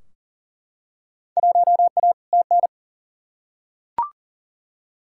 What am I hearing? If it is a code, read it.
Morse code, heard as a keyed tone, about 40 wpm, 700 Hz. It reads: 1A TN
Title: Field Day Exchanges - 40wpm